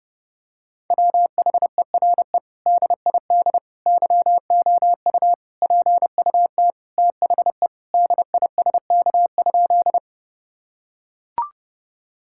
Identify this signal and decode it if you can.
Morse code, heard as a keyed tone, about 32 wpm, 700 Hz. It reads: WHERE DID YOU PUT THE DISK?
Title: Sentences from Top 1000 Words - Review Part 6 - 30wpm